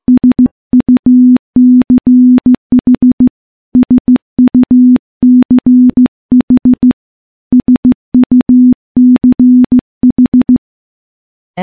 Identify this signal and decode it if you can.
Morse code, heard as a keyed tone, about 15 wpm, 250 Hz. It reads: SUCH SUCH SUCH
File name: morse / morse-training-15WPM